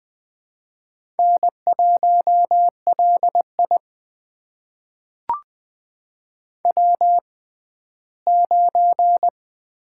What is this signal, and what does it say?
Morse code, heard as a keyed tone, about 21 wpm, spaced as for 14 wpm, 700 Hz. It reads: N1LI W9
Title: USA Call Signs - 4 Characters - 6x Character Spacing - 20wpm